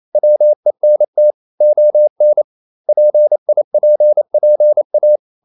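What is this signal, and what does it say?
Morse code, heard as a keyed tone, about 28 wpm, 600 Hz. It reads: WENT ON PIPPA
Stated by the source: Morse code practice at 28 wpm, text of Ken Follett's novel Never in English, from CW med Gnister.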